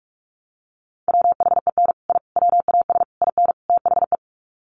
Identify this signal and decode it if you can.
Morse code, heard as a keyed tone, about 45 wpm, 700 Hz. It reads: WHEN I WAS IN THE
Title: Sets of 5 Words Encore2 45wpm